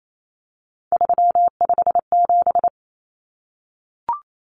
Morse code, heard as a keyed tone, about 28 wpm, 700 Hz. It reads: 357
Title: Common Cut Numbers - 3-Digit 28wpm